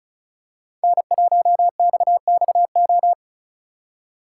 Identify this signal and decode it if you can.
Morse code, heard as a keyed tone, about 36 wpm, 700 Hz. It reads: N1XXO
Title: USA Call Signs - 5 Characters - 2x Character Spacing - 35wpm